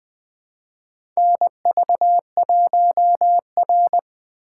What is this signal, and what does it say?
Morse code, heard as a keyed tone, about 20 wpm, 700 Hz. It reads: NV1R
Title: USA Call Signs - 4 Characters - 4x Character Spacing - 20wpm